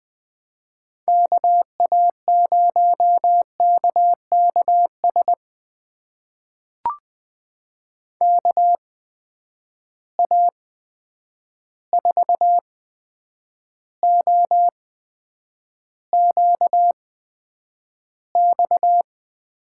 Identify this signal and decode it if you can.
Morse code, heard as a keyed tone, about 20 wpm, spaced as for 9 wpm, 700 Hz. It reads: KA0KKS KA4OQX
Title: USA Call Signs - 8x Character Spacing - 20wpm